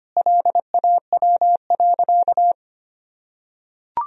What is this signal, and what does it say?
Morse code, heard as a keyed tone, about 25 wpm, 700 Hz. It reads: LAW.
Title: Sentences from Top 1000 Words - Review Part 4 - 25wpm